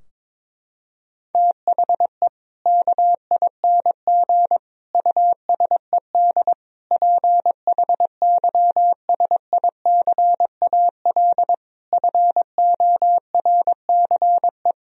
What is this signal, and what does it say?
Morse code, heard as a keyed tone, about 22 wpm, 700 Hz. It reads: THE KING USED PHYSICAL FORCE
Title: Sentences from Top 1000 Words - Review Part 4 - 22wpm